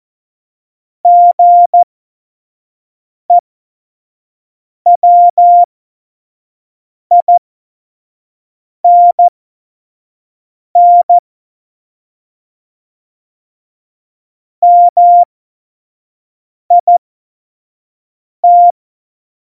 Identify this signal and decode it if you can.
Morse code, heard as a keyed tone, about 14 wpm, spaced as for 5 wpm, 700 Hz. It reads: GEWINN MIT
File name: Morsekurs / Witze / Witz 15. Januar 2019, 25 F70